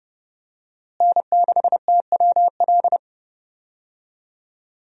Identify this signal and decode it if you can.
Morse code, heard as a keyed tone, about 30 wpm, 700 Hz. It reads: N6TWL
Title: USA Call Signs - 5 Characters - Repeated 3x - 30wpm